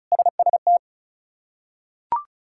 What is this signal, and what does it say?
Morse code, heard as a keyed tone, about 34 wpm, 700 Hz. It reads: SST
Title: K1USN SST QSO Practice - 35wpm